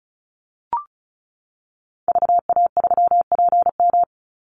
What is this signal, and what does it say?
Morse code, heard as a keyed tone, about 34 wpm, 700 Hz. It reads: VA3PM
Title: Call Signs - 35wpm